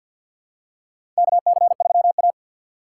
Morse code, heard as a keyed tone, about 49 wpm, 700 Hz. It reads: KC3A
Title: USA Call Signs - 4x Character Spacing - 50wpm